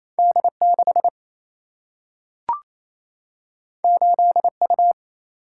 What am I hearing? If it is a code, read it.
Morse code, heard as a keyed tone, about 28 wpm, 700 Hz. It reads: D6 8U
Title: Single-Letter Single-Number 28wpm